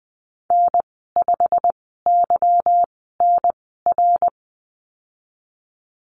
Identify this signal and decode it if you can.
Morse code, heard as a keyed tone, about 20 wpm, spaced as for 15 wpm, 700 Hz. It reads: N5YNR
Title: 154 Lesson 25g - Letter p - Call Signs 20wpm